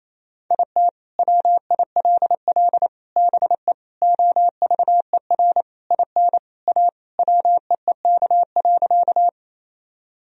Morse code, heard as a keyed tone, about 28 wpm, 700 Hz. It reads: IT WILL BE OVER IN A WEEK.
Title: Sentences from Top 1000 Words - Review Part 2 - 28wpm